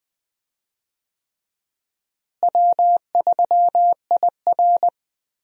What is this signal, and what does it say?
Morse code, heard as a keyed tone, about 20 wpm, 700 Hz. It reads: W3IR
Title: USA Call Signs - 4 Characters - Repeated 3x - 20wpm